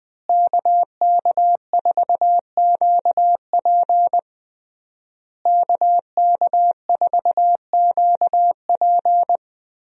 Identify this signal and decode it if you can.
Morse code, heard as a keyed tone, about 20 wpm, 700 Hz. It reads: KK4QP KK4QP
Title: USA Call Signs - 5 Characters - Repeated 3x - 20wpm